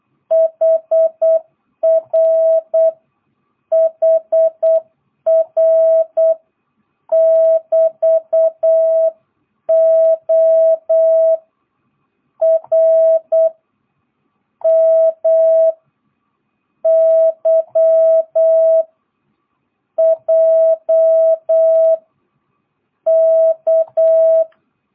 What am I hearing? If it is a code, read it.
Morse code, heard as a keyed tone, about 8 wpm, 660 Hz. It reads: HR HR <BT> O R M Y J K
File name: jh8cbh / 2022cw